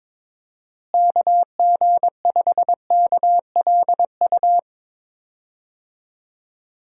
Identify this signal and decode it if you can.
Morse code, heard as a keyed tone, about 22 wpm, 700 Hz. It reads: KG5KLU
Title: USA Call Signs - 6 Characters - Repeated 2x - 22wpm